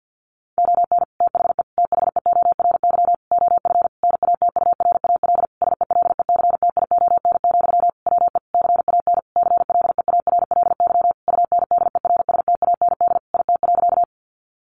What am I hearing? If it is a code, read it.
Morse code, heard as a keyed tone, about 50 wpm, 700 Hz. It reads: ON THE THEORY OF NATURAL SELECTION, WE CAN CLEARLY UNDERSTAND IT.
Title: Sentences from Top 1400 Words 50wpm